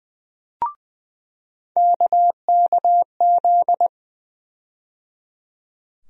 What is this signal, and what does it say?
Morse code, heard as a keyed tone, about 21 wpm, 700 Hz. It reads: KKZ
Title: USA Call Sign Postfixes - 3 Letters 20wpm